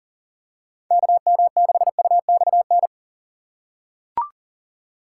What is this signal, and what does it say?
Morse code, heard as a keyed tone, about 37 wpm, 700 Hz. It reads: KM6UXN
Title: USA Call Signs - 2x Character Spacing - 40wpm